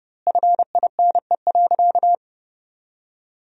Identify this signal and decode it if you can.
Morse code, heard as a keyed tone, about 32 wpm, 700 Hz. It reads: FINE.
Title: 2x Word-Spacing - Sentences from Top 500 Words - 30wpm